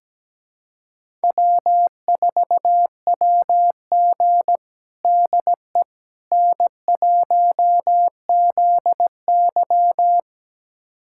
Morse code, heard as a keyed tone, about 17 wpm, 700 Hz. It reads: W4WG DE N1ZY
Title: USA Callsign DE Callsign - 4 Characters - 17wpm